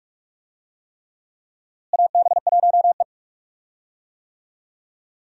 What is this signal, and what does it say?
Morse code, heard as a keyed tone, about 44 wpm, 700 Hz. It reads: AB1E
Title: USA Call Signs - 4 Characters - Repeated 3x - 45wpm